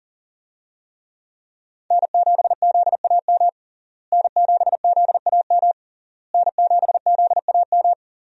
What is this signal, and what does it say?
Morse code, heard as a keyed tone, about 37 wpm, 700 Hz. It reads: N7ZAM N7ZAM N7ZAM
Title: USA Call Signs - 5 Characters - Repeated 3x - 40wpm